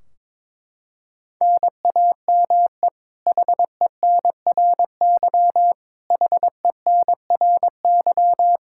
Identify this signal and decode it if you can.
Morse code, heard as a keyed tone, about 22 wpm, 700 Hz. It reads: NAME HENRY HENRY
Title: Practice QSO - Exchange 1 - Beginning - 22wpm